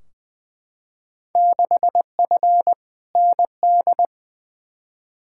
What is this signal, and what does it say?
Morse code, heard as a keyed tone, about 21 wpm, 700 Hz. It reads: 6F ND
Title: Field Day Exchanges Encore - 20wpm